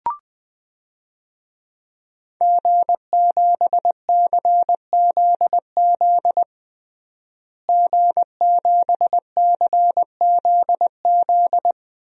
Morse code, heard as a keyed tone, about 20 wpm, 700 Hz. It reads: G7CZZ G7CZZ
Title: DXCC-100 Call Signs - 5 Characters - Repeated 2x - 20wpm